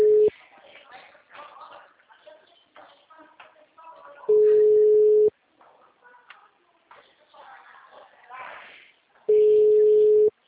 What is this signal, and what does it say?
Ringback tone (Europe, 425 Hz), with speech.